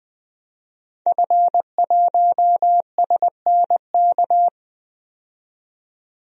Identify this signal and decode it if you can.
Morse code, heard as a keyed tone, about 20 wpm, 700 Hz. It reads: F1SNK
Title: DXCC-100 Call Signs - 5 Characters - 20wpm